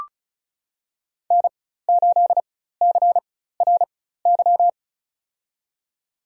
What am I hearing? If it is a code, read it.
Morse code, heard as a keyed tone, about 36 wpm, spaced as for 16 wpm, 700 Hz. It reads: N8CRY
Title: USA Call Signs - 5 Characters - 4x Character Spacing - 35wpm